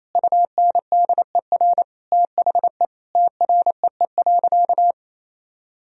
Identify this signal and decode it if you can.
Morse code, heard as a keyed tone, about 28 wpm, 700 Hz. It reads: UNDER THE TREE.
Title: Sentences from Top 1000 Words - Review Part 1 - 28wpm